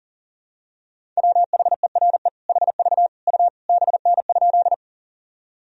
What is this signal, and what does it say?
Morse code, heard as a keyed tone, about 40 wpm, 700 Hz. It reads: WHERE HV U BN?
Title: Sentences from Top 200 Words and Abbreviations 40wpm